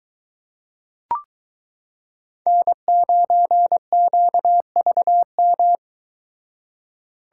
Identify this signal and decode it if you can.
Morse code, heard as a keyed tone, about 23 wpm, 700 Hz. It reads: N9QVM